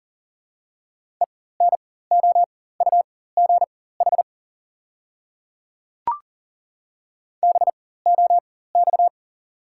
Morse code, heard as a keyed tone, about 37 wpm, spaced as for 18 wpm, 690 Hz. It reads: ENOUGH BOX
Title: 4x Character Spacing - Top 500 Words - 40wpm